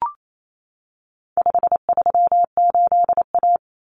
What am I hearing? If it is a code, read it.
Morse code, heard as a keyed tone, about 28 wpm, 700 Hz. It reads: 538A
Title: Common Cut Numbers - 4-Digit 28wpm